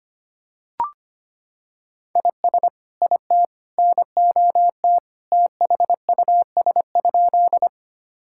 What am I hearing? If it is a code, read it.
Morse code, heard as a keyed tone, about 25 wpm, 700 Hz. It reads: IS IT NOT THUS?
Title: Sentences from Top 1000 Words - Review Part 5 - 25wpm